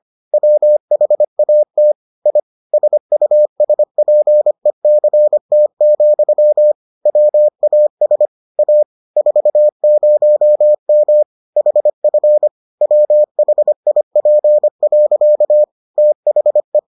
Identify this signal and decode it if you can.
Morse code, heard as a keyed tone, about 25 wpm, 600 Hz. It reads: WHAT I SUSPECT, WAS A 40M HF WHIP. THE